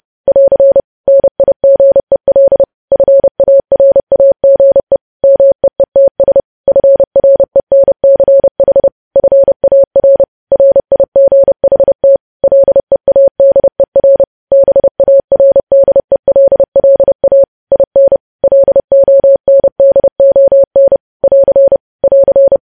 Morse code, heard as a keyed tone, about 30 wpm, 550 Hz. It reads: <AR> NIGEL FARAGE MEETS FRENCH FAR RIGHT LEADER BARDELLA IN LONDON <AR> <AR>